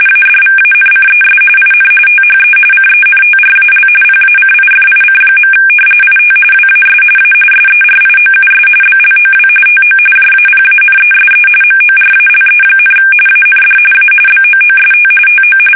RTTY, about 45.45 baud. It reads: ADVANCES IN THE CENTRAL PACIFIC. IT WAS A SERIOUS BLOW TO JAPANESE MORALE, AND A BOOST TO
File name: fsk850 / battleshipnj